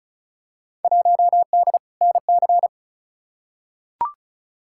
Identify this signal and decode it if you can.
Morse code, heard as a keyed tone, about 36 wpm, 700 Hz. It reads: 1D NC
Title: Field Day Exchanges - 35wpm